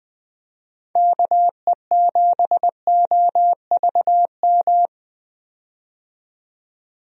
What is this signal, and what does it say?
Morse code, heard as a keyed tone, about 20 wpm, 700 Hz. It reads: KE7OVM
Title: USA Call Signs - 6 Characters - 20wpm